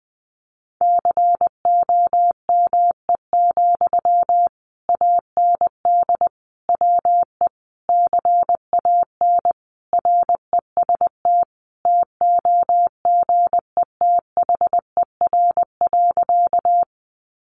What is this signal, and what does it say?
Morse code, heard as a keyed tone, about 20 wpm, 700 Hz. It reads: COME, AND WE CAN REST TOGETHER.
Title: Sentences from Top 500 Words 20wpm